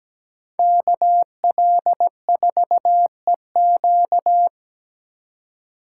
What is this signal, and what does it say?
Morse code, heard as a keyed tone, about 18 wpm, 700 Hz. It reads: KL4EQ
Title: Speed Racing - USA 2x2 Call Signs 17wpm